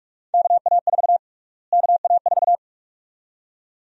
Morse code, heard as a keyed tone, about 47 wpm, 700 Hz. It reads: KA4 KA4
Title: USA Call Sign Prefixes - Repeated 2x - 45wpm